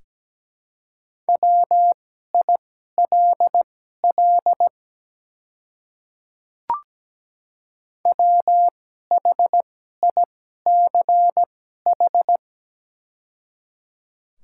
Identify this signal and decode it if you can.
Morse code, heard as a keyed tone, about 17 wpm, spaced as for 12 wpm, 690 Hz. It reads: WILL WHICH
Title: 2x Character Spacing - Top 100 Words - 17wpm